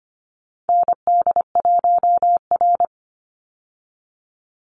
Morse code, heard as a keyed tone, about 25 wpm, 700 Hz. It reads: ND1R
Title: Call Signs - 25wpm